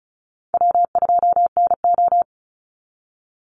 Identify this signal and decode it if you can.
Morse code, heard as a keyed tone, about 34 wpm, 700 Hz. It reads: W2NO
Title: Call Signs Repeated 3x - 35wpm